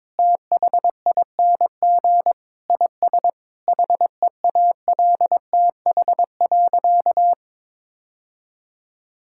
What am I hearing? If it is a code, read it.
Morse code, heard as a keyed tone, about 22 wpm, 700 Hz. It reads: THING IS HEALTH.
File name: practice / 22wpm / Sentences with 3-Word Phrases - 22wpm